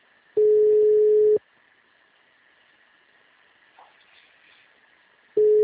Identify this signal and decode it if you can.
Ringback tone (Europe, 425 Hz).